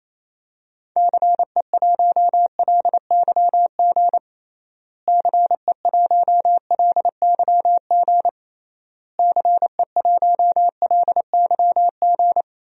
Morse code, heard as a keyed tone, about 28 wpm, 700 Hz. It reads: CE1LYG CE1LYG CE1LYG
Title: DXCC-100 Call Signs - 6 Characters - Repeated 3x - 28wpm